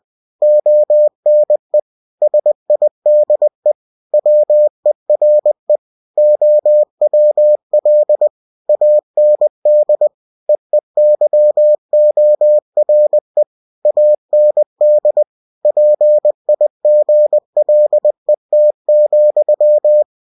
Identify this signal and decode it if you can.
Morse code, heard as a keyed tone, about 20 wpm, 600 Hz. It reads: ONE SIDE WERE OWL AND EEYORE AND PIGLET,
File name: cw-books / winnie-the-pooh / winnie-20wpm-page-0106